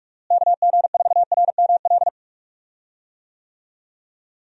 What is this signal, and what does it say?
Morse code, heard as a keyed tone, about 46 wpm, 700 Hz. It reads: KG4RML
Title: USA Call Signs - 6 Characters - Repeated 2x - 45wpm